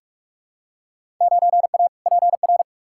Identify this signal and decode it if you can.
Morse code, heard as a keyed tone, about 42 wpm, 700 Hz. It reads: 9A PR
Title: Field Day Exchanges Encore - 45wpm